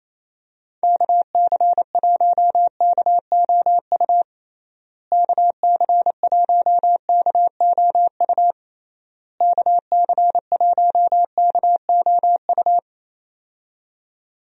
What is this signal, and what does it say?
Morse code, heard as a keyed tone, about 28 wpm, 700 Hz. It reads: KC1KOU KC1KOU KC1KOU
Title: USA Call Signs - 6 Characters - Repeated 3x - 28wpm